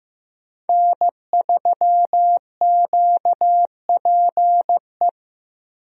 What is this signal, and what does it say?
Morse code, heard as a keyed tone, about 15 wpm, 700 Hz. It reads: N3QPE
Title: USA Call Signs - 5 Characters - Repeated 2x - 15wpm